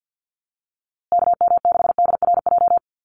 Morse code, heard as a keyed tone, about 50 wpm, 700 Hz. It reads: KM6DRJ
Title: USA Call Signs Repeated 3x 50wpm